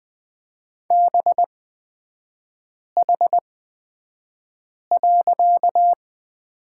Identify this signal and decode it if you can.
Morse code, heard as a keyed tone, about 20 wpm, spaced as for 5 wpm, 700 Hz. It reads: BH.
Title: 255 Lesson 41e - Period - Three Characters 20wpm